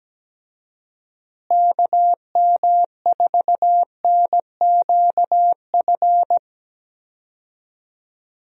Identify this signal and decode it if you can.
Morse code, heard as a keyed tone, about 17 wpm, 700 Hz. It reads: KM4NQF
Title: USA Call Signs - 6 Characters - Repeated 3x - 17wpm